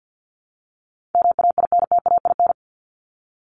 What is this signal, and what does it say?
Morse code, heard as a keyed tone, about 50 wpm, 700 Hz. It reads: MAINTAIN
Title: Top 501-750 Words 50wpm